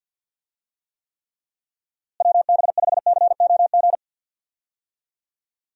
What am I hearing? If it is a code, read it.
Morse code, heard as a keyed tone, about 51 wpm, 700 Hz. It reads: WB5COG
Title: USA Call Signs - 6 Characters - Repeated 3x - 50wpm